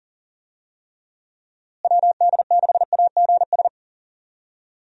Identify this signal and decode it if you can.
Morse code, heard as a keyed tone, about 40 wpm, 700 Hz. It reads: WD6AGS
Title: USA Call Signs - 6 Characters - Repeated 3x - 40wpm